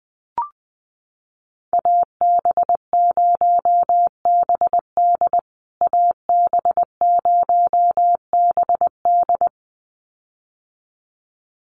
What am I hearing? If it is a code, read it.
Morse code, heard as a keyed tone, about 20 wpm, 700 Hz. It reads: AB0BD AB0BD
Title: USA Call Signs Repeated 2x 20wpm